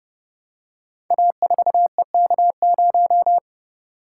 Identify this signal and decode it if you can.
Morse code, heard as a keyed tone, about 30 wpm, 700 Hz. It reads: A4EK0
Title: Code Groups - 5 Letter:Numbers - 30wpm